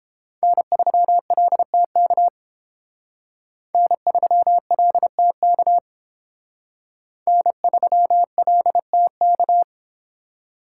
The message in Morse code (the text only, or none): N3LTK N3LTK N3LTK